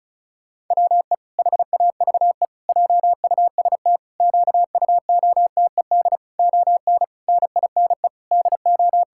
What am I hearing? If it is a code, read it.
Morse code, heard as a keyed tone, about 36 wpm, 700 Hz. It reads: WE HAVE JUST QUOTED ON NINE DO
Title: Pangram Challenge v2 - 35wpm